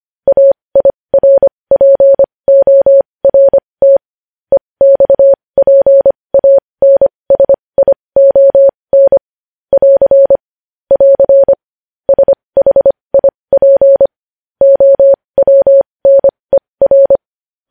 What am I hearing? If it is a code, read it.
Morse code, heard as a keyed tone, about 25 wpm, spaced as for 20 wpm, 550 Hz. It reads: AIRPORT EXPANSION <AR> <AR> SHIP OWNER
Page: News Headlines in Morse Code at 20 WPM